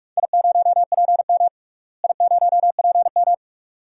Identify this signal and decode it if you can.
Morse code, heard as a keyed tone, about 46 wpm, 700 Hz. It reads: I0PM I0PM
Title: DXCC-100 Call Signs - 4 Characters - Repeated 2x - 45wpm